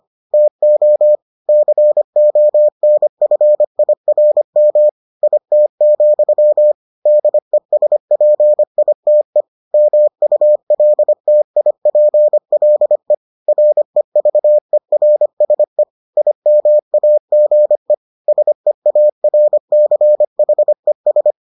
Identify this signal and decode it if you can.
Morse code, heard as a keyed tone, about 25 wpm, 600 Hz. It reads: TO CONFIRM IT, DESPITE MULTIPLE REVERSE IMAGE SEARCHES